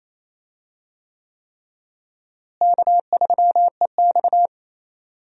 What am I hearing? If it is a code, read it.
Morse code, heard as a keyed tone, about 28 wpm, 700 Hz. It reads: K3EX
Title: USA Call Signs - 4 Characters - Repeated 3x - 28wpm